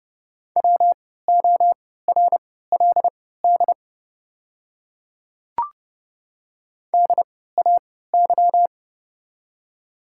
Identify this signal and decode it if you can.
Morse code, heard as a keyed tone, about 30 wpm, spaced as for 17 wpm, 690 Hz. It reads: WORLD DAY
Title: 3x Character Spacing - Top 200 Words - 30wpm